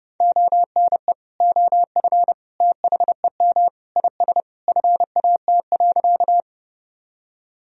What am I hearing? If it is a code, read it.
Morse code, heard as a keyed tone, about 30 wpm, 700 Hz. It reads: ONE OF THEM IS FAT.
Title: Sentences from Top 1000 Words - Review Part 4 - 30wpm